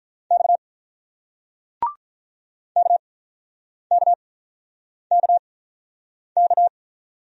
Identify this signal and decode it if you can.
Morse code, heard as a keyed tone, about 47 wpm, spaced as for 9 wpm, 700 Hz. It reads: X KKKK